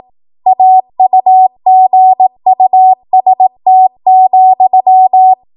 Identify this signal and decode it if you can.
Morse code, heard as a keyed tone, about 18 wpm, 750 Hz. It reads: AUGUST,
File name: Morse / 18WPM